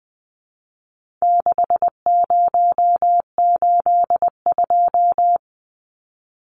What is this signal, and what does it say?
Morse code, heard as a keyed tone, about 20 wpm, 700 Hz. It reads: 6082
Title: Numbers - 4-Digit 20wpm